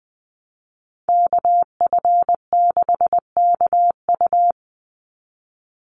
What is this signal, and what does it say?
Morse code, heard as a keyed tone, about 20 wpm, 700 Hz. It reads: KF6KU
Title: USA Call Signs 20wpm